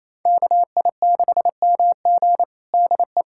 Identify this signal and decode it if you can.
Morse code, heard as a keyed tone, about 28 wpm, 700 Hz. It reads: KI6MG DE
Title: USA Callsign DE Callsign 28wpm